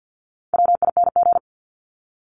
Morse code, heard as a keyed tone, about 50 wpm, 700 Hz. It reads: WING
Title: Top 1001-1250 Words 50wpm